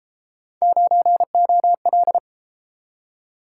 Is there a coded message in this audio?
Yes: Morse code from 0.6 to 2.2 s, heard as a keyed tone, about 32 wpm, 700 Hz. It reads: 9OL